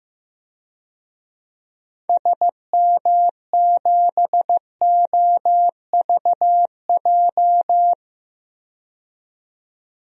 Morse code, heard as a keyed tone, about 16 wpm, 700 Hz. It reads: SM7OVJ
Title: DXCC-100 Call Signs - 6 Characters - 15wpm